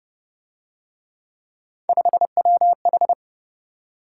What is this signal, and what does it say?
Morse code, heard as a keyed tone, about 30 wpm, 700 Hz. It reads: 5WH
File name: Speed-Racing - Three Letter-Number Combinations - 25wpm